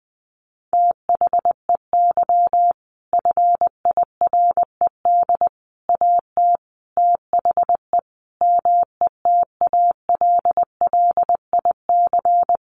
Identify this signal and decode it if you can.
Morse code, heard as a keyed tone, about 20 wpm, 700 Hz. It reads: THEY FIRED AT THE METALLIC
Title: A Sound Of Thunder 20wpm